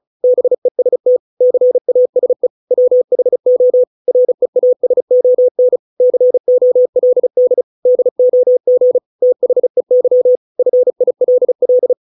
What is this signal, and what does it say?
Morse code, heard as a keyed tone, about 34 wpm, 500 Hz. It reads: BEST CASE WHO REASON COLD DOG THEY FILL